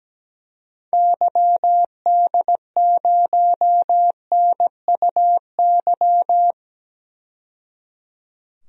Morse code, heard as a keyed tone, about 17 wpm, 700 Hz. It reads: YD0NUY
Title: DXCC-100 Call Signs - 6 Characters - 17wpm